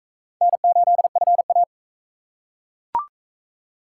Morse code, heard as a keyed tone, about 39 wpm, 700 Hz. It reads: N8FA